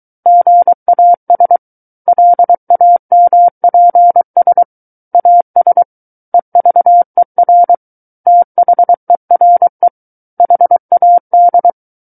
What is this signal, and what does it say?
Morse code, heard as a keyed tone, about 23 wpm, 700 Hz. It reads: GAS LAMPS AS EVER THERE HAD
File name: Book One / 14-Chapter 14 - 23 WPM